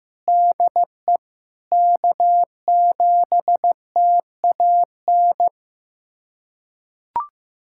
Morse code, heard as a keyed tone, about 15 wpm, 700 Hz. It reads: DE K7TAN
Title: USA Callsign DE Callsign - 5 Characters - 15wpm